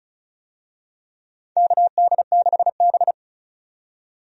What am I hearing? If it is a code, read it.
Morse code, heard as a keyed tone, about 36 wpm, 700 Hz. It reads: KD6B
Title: USA Call Signs - 4 Characters - 35wpm